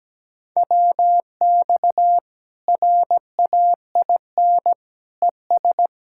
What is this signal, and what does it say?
Morse code, heard as a keyed tone, about 17 wpm, 700 Hz. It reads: WX RAIN ES
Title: Practice QSO - Exchange 2 - Info - 17wpm